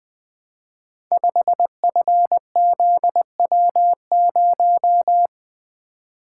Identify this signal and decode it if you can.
Morse code, heard as a keyed tone, about 20 wpm, 700 Hz. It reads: 5FZW0
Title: Code Groups - 5 Letter:Numbers - 20wpm